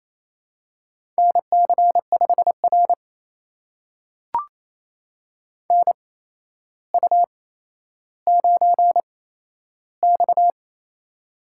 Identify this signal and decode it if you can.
Morse code, heard as a keyed tone, about 27 wpm, spaced as for 12 wpm, 700 Hz. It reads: NC5R NU9X
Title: USA Call Signs - 4 Characters - 8x Character Spacing - 28wpm